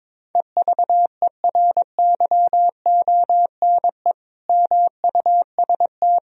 Morse code, heard as a keyed tone, about 22 wpm, 700 Hz. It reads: EVERYONE MUST
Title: Sentences from Top 1000 Words - Review Part 4 - 22wpm